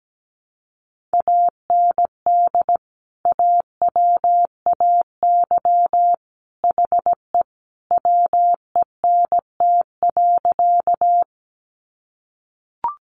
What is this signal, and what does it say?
Morse code, heard as a keyed tone, about 17 wpm, 700 Hz. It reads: AND AWAY HE WENT.
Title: Sentences from Top 600 Words 17wpm